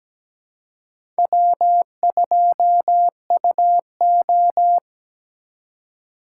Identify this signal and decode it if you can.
Morse code, heard as a keyed tone, about 17 wpm, 700 Hz. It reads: W2UO
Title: USA Call Signs - 2x Character Spacing - 17wpm